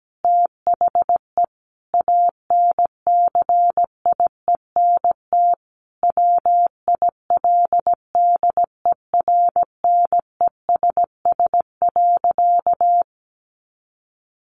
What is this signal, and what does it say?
Morse code, heard as a keyed tone, about 17 wpm, 700 Hz. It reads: THE ANCIENT WILDERNESS.
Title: A Sound Of Thunder 17wpm